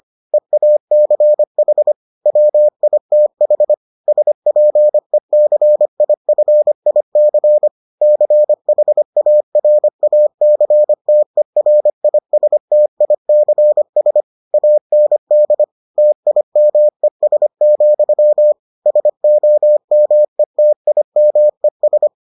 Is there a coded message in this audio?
Yes: Morse code from 0.3 to 22.1 s, heard as a keyed tone, about 25 wpm, 600 Hz. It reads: EACH WITH SPECIFIC CHARACTERISTICS AND TIMES, SOMETIMES